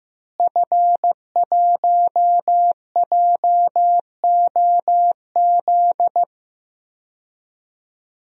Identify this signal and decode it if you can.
Morse code, heard as a keyed tone, about 15 wpm, 700 Hz. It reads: F1JOZ
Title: DXCC-100 Call Signs - 5 Characters - Repeated 3x - 15wpm